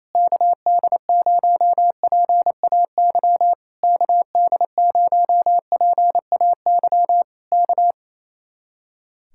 Morse code, heard as a keyed tone, about 27 wpm, 700 Hz. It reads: KD0PAY KD0PAY K
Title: Speed-Racing Common QSO Phrases - 28wpm